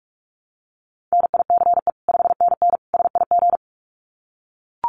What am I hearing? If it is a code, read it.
Morse code, heard as a keyed tone, about 44 wpm, 700 Hz. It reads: NICE 5NN SIG
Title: Common QSO Phrases 45wpm